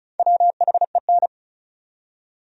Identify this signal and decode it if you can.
Morse code, heard as a keyed tone, about 36 wpm, 700 Hz. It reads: WHEN
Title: Top 100 Words 35wpm